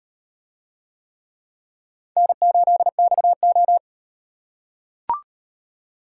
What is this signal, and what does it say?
Morse code, heard as a keyed tone, about 33 wpm, 700 Hz. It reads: N8XO